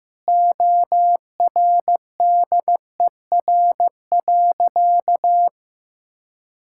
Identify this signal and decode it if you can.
Morse code, heard as a keyed tone, about 15 wpm, 700 Hz. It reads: ORDER.
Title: Sentences from Top 1600 Words 15wpm